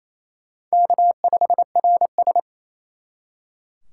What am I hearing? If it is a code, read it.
Morse code, heard as a keyed tone, about 36 wpm, 700 Hz. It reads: K5RS